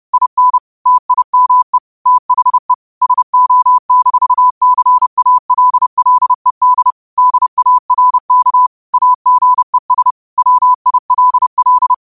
Morse code, heard as a keyed tone, about 30 wpm, 1000 Hz. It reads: IN TIME THE SO-CALLED DARK AGES WILL